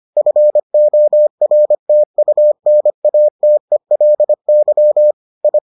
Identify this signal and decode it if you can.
Morse code, heard as a keyed tone, about 25 wpm, 600 Hz. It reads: FORTUNATELY I